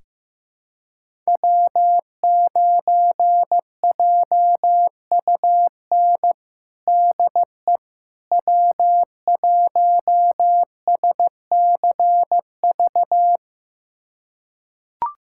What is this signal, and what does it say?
Morse code, heard as a keyed tone, about 16 wpm, 700 Hz. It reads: W9JUN DE W1SCV
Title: USA Callsign DE Callsign - 5 Characters - 15wpm